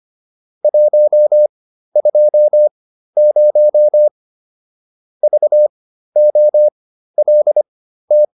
Morse code, heard as a keyed tone, about 25 wpm, spaced as for 13 wpm, 600 Hz. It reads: 120 VOLT